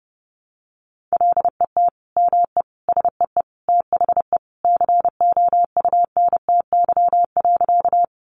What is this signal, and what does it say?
Morse code, heard as a keyed tone, about 30 wpm, 700 Hz. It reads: LET ME SEE THE COUNTY.
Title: Sentences from Top 1200 Words 30wpm